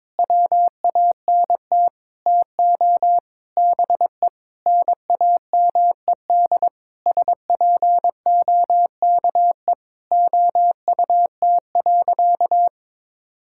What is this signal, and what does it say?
Morse code, heard as a keyed tone, about 22 wpm, 700 Hz. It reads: WANT TO BE NAMED SPOKE OUT.
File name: Sentences with 7-Word Phrases - 22wpm